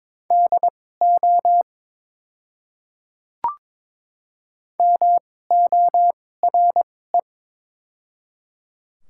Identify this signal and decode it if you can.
Morse code, heard as a keyed tone, about 22 wpm, spaced as for 16 wpm, 700 Hz. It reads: DO MORE
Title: 2x Character Spacing - Top 100 Words - 22wpm